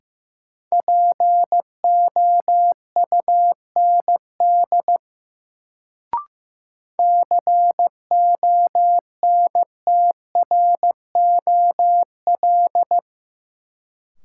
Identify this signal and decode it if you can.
Morse code, heard as a keyed tone, about 15 wpm, 700 Hz. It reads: POUND CONTROL
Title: Sentences from Top 1000 Words - Review Part 2 - New Word Drill - 15wpm